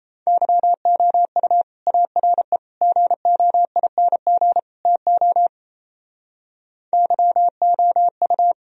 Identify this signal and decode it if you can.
Morse code, heard as a keyed tone, about 35 wpm, 700 Hz. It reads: YOU ARE GOING TO YOU
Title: Speed-Racing - Sets of 4 Words 28wpm